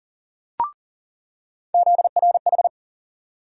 Morse code, heard as a keyed tone, about 40 wpm, 700 Hz. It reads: ZRH
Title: USA Call Sign Postfixes - 3 Letters 40wpm